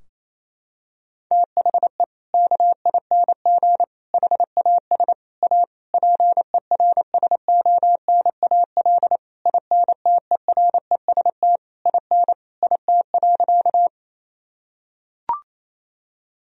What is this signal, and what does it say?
Morse code, heard as a keyed tone, about 28 wpm, 700 Hz. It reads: THE KING HAS A PERSONAL INTEREST IN IT.
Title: Sentences from Top 1000 Words - Review Part 4 - 28wpm